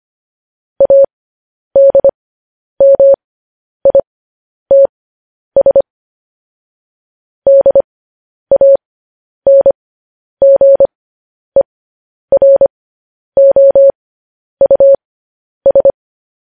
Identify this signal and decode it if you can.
Morse code, heard as a keyed tone, about 25 wpm, spaced as for 10 wpm, 550 Hz. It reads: ADMITS DANGEROUS
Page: news headlines in morse code at 10 wpm